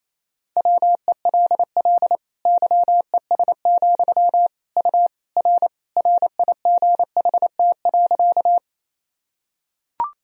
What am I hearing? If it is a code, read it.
Morse code, heard as a keyed tone, about 28 wpm, 700 Hz. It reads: WELL YES, U R RIGHT.
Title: Sentences from Top 500 Words and Abbreviations 28wpm